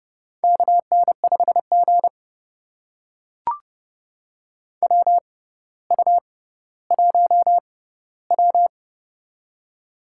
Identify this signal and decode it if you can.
Morse code, heard as a keyed tone, about 30 wpm, spaced as for 15 wpm, 700 Hz. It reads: KN5G WU1W
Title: USA Call Signs - 4 Characters - 6x Character Spacing - 30wpm